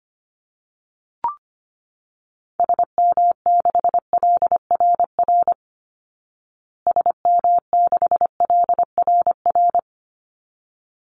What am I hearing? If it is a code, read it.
Morse code, heard as a keyed tone, about 25 wpm, 700 Hz. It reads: SM6LRR SM6LRR
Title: Call Signs Repeated 3x - 25wpm